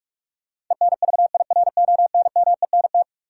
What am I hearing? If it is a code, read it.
Morse code, heard as a keyed tone, about 46 wpm, 700 Hz. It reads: ENVIRONMENT
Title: Sentences from Top 1000 Words - Review Part 5 - New Word Drill - 45wpm